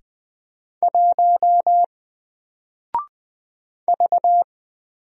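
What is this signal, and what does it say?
Morse code, heard as a keyed tone, about 20 wpm, 700 Hz. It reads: 1 V
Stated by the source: Single letter and number Morse drill at 20 wpm, from N4LCD.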